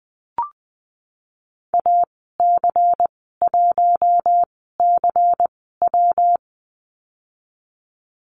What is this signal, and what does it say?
Morse code, heard as a keyed tone, about 20 wpm, spaced as for 14 wpm, 700 Hz. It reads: AC1CW
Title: 128 Lesson 21g - Number 6 - Call Signs 20wpm